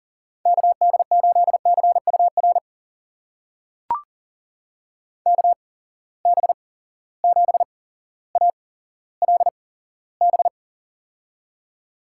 Morse code, heard as a keyed tone, about 40 wpm, spaced as for 17 wpm, 700 Hz. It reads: KD8CUR KB7ALB
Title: USA Call Signs - 6 Characters - 8x Character Spacing - 40wpm